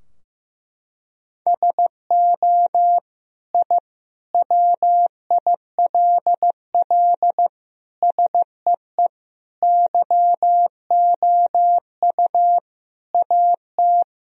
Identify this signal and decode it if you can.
Morse code, heard as a keyed tone, about 16 wpm, 700 Hz. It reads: SO I WILL SEE YOU AT
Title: Sentences from Top 1000 Words - Review Part 2 - 15wpm